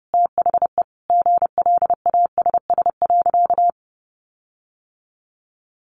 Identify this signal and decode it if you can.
Morse code, heard as a keyed tone, about 30 wpm, 700 Hz. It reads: THE GLASS.
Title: Sentences from Top 1300 Words 30wpm